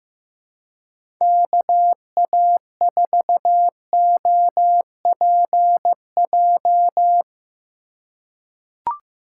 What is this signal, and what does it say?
Morse code, heard as a keyed tone, about 16 wpm, 700 Hz. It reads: KA4OPJ
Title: USA Call Signs - 3x Character Spacing - 15wpm